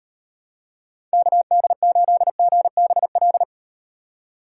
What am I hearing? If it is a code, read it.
Morse code, heard as a keyed tone, about 38 wpm, 700 Hz. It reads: KD8GBL